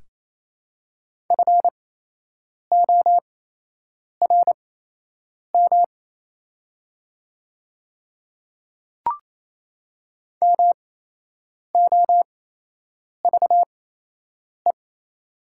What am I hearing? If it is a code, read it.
Morse code, heard as a keyed tone, about 28 wpm, spaced as for 8 wpm, 700 Hz. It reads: FORM MOVE
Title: 8x Character Spacing - Top 500 Words - 28wpm